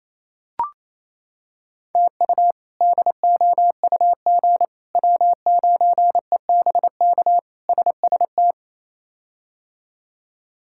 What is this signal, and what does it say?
Morse code, heard as a keyed tone, about 27 wpm, 700 Hz. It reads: TU DOUG W9EBK SST
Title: K1USN SST QSO Practice - 28wpm